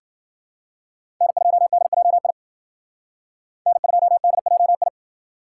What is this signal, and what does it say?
Morse code, heard as a keyed tone, about 51 wpm, 700 Hz. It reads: N2DJI N2DJI